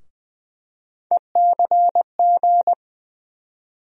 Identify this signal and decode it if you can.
Morse code, heard as a keyed tone, about 20 wpm, 700 Hz. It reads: ECG
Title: USA Call Sign Postfixes - 3 Letters 20wpm